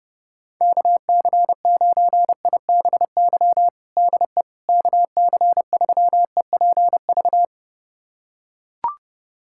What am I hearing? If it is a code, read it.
Morse code, heard as a keyed tone, about 30 wpm, 700 Hz. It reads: KC9IBY DE KC3EPV
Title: USA Callsign DE Callsign - 6 Characters - 30wpm